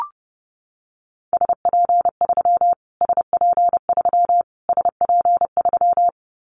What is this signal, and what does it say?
Morse code, heard as a keyed tone, about 30 wpm, 700 Hz. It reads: SP3 SP3 SP3
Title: Callsign Prefixes for Top 25 DXCC - Repeated 3x - 30wpm